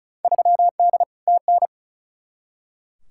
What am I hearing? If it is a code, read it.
Morse code, heard as a keyed tone, about 34 wpm, 700 Hz. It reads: 3D TN
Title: Field Day Exchanges Encore - 35wpm